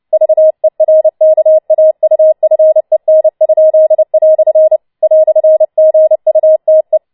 Morse code, heard as a keyed tone, about 29 wpm, 630 Hz. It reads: VERKAUFEN?" "GUTE